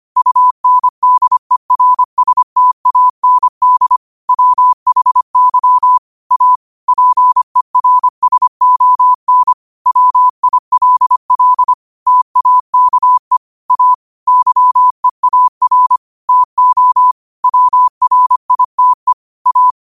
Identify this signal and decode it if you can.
Morse code, heard as a keyed tone, about 25 wpm, 1000 Hz. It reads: UNDERSTAND WHY A PERSON WILL TAKE A YEAR TO WRITE A